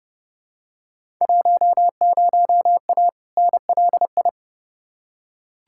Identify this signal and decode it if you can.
Morse code, heard as a keyed tone, about 30 wpm, 700 Hz. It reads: 10A NLI
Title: Field Day Exchanges Encore - 30wpm